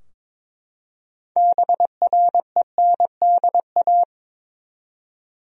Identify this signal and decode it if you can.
Morse code, heard as a keyed tone, about 22 wpm, 700 Hz. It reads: BRENDA
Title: USA Common Names - 22wpm